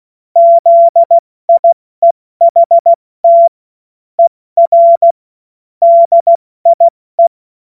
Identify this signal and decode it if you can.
Morse code, heard as a keyed tone, about 16 wpm, 680 Hz. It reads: ZIEHT ER DIE
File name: Morsekurs / Witze / Witz 31.Oktober 2023 70 F80